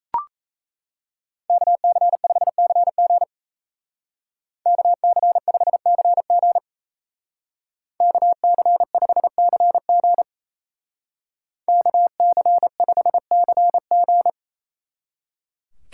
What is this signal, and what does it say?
Morse code, heard as a keyed tone, about 36 wpm, 700 Hz. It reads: KC5CG KC5CG KC5CG KC5CG